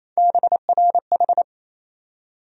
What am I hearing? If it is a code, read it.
Morse code, heard as a keyed tone, about 28 wpm, 700 Hz. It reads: BRH
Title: USA Call Sign Postfixes - 3 Letters 28wpm